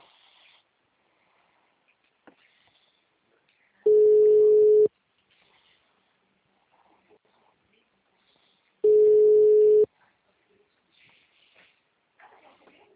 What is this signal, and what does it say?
Ringback tone (Europe, 425 Hz).